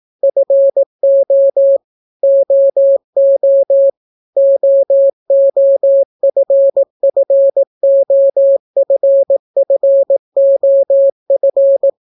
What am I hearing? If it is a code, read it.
Morse code, heard as a keyed tone, about 18 wpm, 550 Hz. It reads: FO OO OOFFOFFOF